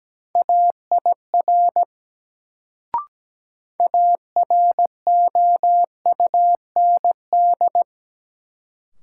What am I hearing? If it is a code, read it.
Morse code, heard as a keyed tone, about 17 wpm, 700 Hz. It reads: AIR AROUND
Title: Sets of 2 Words - New Word Drill - 17wpm